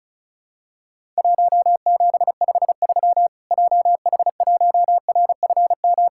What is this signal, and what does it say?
Morse code, heard as a keyed tone, about 36 wpm, 700 Hz. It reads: 1753 JH1RFM
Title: Worked All Europe DX Contest - QTC Exchange - 35wpm